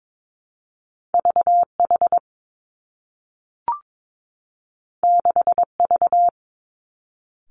Morse code, heard as a keyed tone, about 22 wpm, 700 Hz. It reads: V H 6 V
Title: Two Challenging Letter-Number Combinations 22wpm